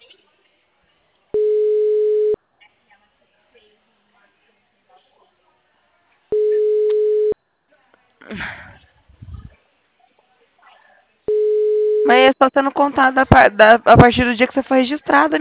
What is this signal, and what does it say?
Ringback tone (Europe, 425 Hz), with speech.
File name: out- -2006- - - .4